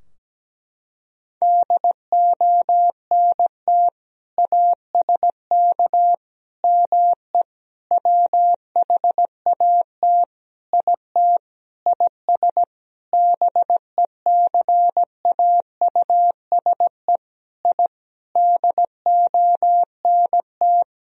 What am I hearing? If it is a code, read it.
Morse code, heard as a keyed tone, about 17 wpm, 700 Hz. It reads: DONT ASK ME WHAT IT IS BECAUSE I DONT
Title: Sentences from Top 1000 Words - Review Part 3 - 17wpm